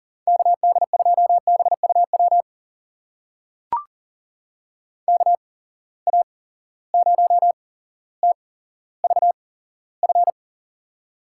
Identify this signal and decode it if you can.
Morse code, heard as a keyed tone, about 38 wpm, spaced as for 17 wpm, 700 Hz. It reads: KD2BUW KA0TVF
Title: USA Call Signs - 6 Characters - 8x Character Spacing - 40wpm